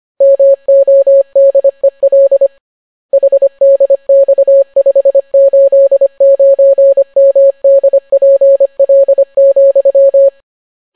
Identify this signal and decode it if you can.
Morse code, heard as a keyed tone, about 25 wpm, 550 Hz. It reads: MODEL HDX589MDPL,